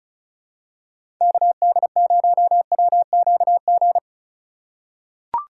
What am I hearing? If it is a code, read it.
Morse code, heard as a keyed tone, about 36 wpm, 700 Hz. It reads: KD0WQG
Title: USA Call Signs - 6 Characters - 4x Character Spacing - 35wpm